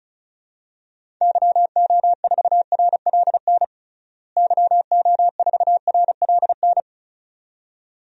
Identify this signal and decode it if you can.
Morse code, heard as a keyed tone, about 34 wpm, 700 Hz. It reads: YO4RLN YO4RLN
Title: DXCC-100 Call Signs - 6 Characters - Repeated 2x - 35wpm